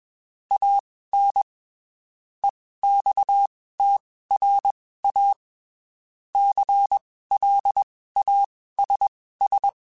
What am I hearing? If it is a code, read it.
Morse code, heard as a keyed tone, about 21 wpm, spaced as for 15 wpm, 800 Hz. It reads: AN EXTRA CLASS